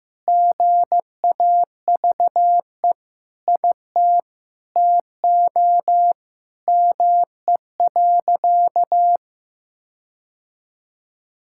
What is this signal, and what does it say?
Morse code, heard as a keyed tone, about 15 wpm, 700 Hz. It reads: GAVE IT TO ME.
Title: Sentences from Top 1600 Words 15wpm